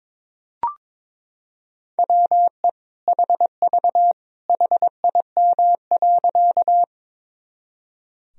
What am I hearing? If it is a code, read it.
Morse code, heard as a keyed tone, about 21 wpm, 700 Hz. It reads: WE HV HIM.
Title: Sentences from Top 100 Words and Abbreviations 22wpm